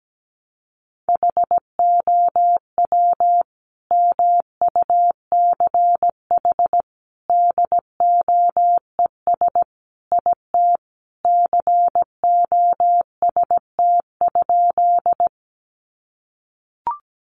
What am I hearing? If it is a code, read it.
Morse code, heard as a keyed tone, about 17 wpm, 700 Hz. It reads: HOW MUCH DOES IT COST?
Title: Sentences from Top 800 Words 17wpm